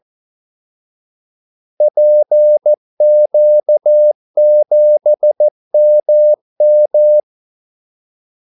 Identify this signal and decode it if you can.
Morse code, heard as a keyed tone, about 14 wpm, 600 Hz. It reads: PQ7MM
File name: Callsigns-14wpm